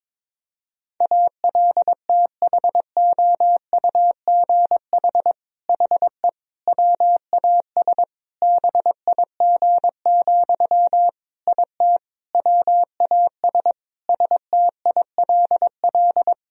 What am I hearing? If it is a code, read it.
Morse code, heard as a keyed tone, about 22 wpm, 700 Hz. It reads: ALTHOUGH HE WAS BIG, IT WAS STILL
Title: Sentences from Top 1000 Words - Review Part 3 - 22wpm